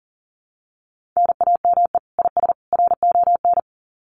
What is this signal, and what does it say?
Morse code, heard as a keyed tone, about 40 wpm, 700 Hz. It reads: NAME IS RON
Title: Common QSO Phrases 40wpm